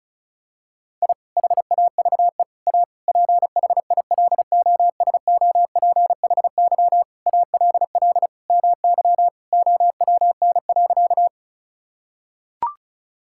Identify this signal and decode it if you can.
Morse code, heard as a keyed tone, about 36 wpm, 700 Hz. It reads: I HAVE A PHILOSOPHY ALL MY OWN.
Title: Sentences from Top 1000 Words - Review Part 6 - 35wpm